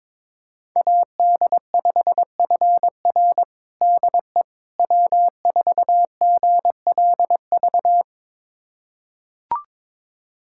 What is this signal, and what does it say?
Morse code, heard as a keyed tone, about 21 wpm, 700 Hz. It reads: AD5FR DE W4GLV
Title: USA Callsign DE Callsign - 5 Characters - 22wpm